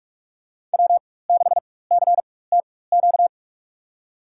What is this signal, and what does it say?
Morse code, heard as a keyed tone, about 46 wpm, spaced as for 21 wpm, 700 Hz. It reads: W6CTQ
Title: USA Call Signs - 5 Characters - 4x Character Spacing - 45wpm